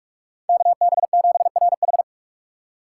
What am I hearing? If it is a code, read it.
Morse code, heard as a keyed tone, about 46 wpm, 700 Hz. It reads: KB7RH
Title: USA Call Signs - 5 Characters - 2x Character Spacing - 45wpm